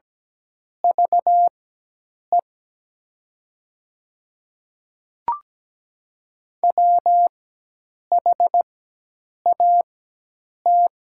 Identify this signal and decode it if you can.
Morse code, heard as a keyed tone, about 17 wpm, spaced as for 8 wpm, 700 Hz. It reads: VE WHAT
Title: 4x Character Spacing - Top 100 Words - 17wpm